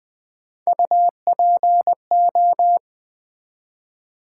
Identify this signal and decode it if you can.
Morse code, heard as a keyed tone, about 20 wpm, 700 Hz. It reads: UPO
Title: Build A Word - 1 Letter At A Time - 4-Letter Words - 20wpm